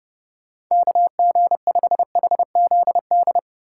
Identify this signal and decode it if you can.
Morse code, heard as a keyed tone, about 31 wpm, 700 Hz. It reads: KG5HZD
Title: USA Call Signs - 6 Characters - 3x Character Spacing - 30wpm